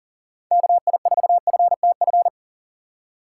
Morse code, heard as a keyed tone, about 40 wpm, 700 Hz. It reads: KI4FTF
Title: USA Call Signs - 6 Characters - 6x Character Spacing - 40wpm